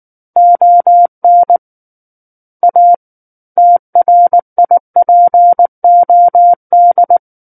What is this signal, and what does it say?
Morse code, heard as a keyed tone, about 19 wpm, 700 Hz. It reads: ON A TRIPOD
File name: Book One / 10-Chapter 10 - 19 WPM